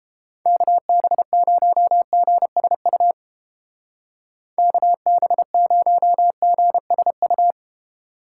Morse code, heard as a keyed tone, about 29 wpm, 700 Hz. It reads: KB0GSU KB0GSU